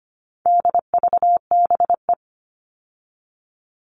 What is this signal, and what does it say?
Morse code, heard as a keyed tone, about 25 wpm, 700 Hz. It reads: DVBE
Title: All Cut Numbers - 4-Digit 25wpm